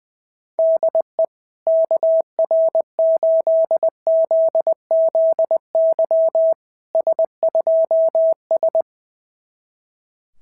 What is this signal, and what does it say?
Morse code, heard as a keyed tone, about 20 wpm, 690 Hz. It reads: DE KR8ZZY S2S
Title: SOTA QSO Practice 20wpm